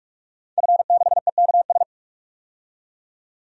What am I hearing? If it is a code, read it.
Morse code, heard as a keyed tone, about 42 wpm, 700 Hz. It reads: F6EKS
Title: DXCC-100 Call Signs - 5 Characters - 45wpm